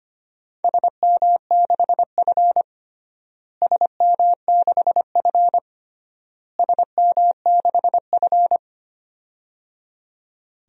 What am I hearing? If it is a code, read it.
Morse code, heard as a keyed tone, about 25 wpm, 700 Hz. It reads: SM6F SM6F SM6F
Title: DXCC-100 Call Signs - 4 Characters - Repeated 3x - 25wpm